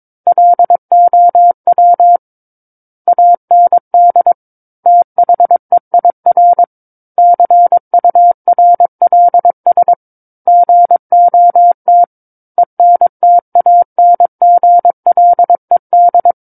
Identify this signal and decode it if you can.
Morse code, heard as a keyed tone, about 22 wpm, 700 Hz. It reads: LOW AND THEIR CURLS GOT ENTANGLED
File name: 06-Chapter-06 22 WPM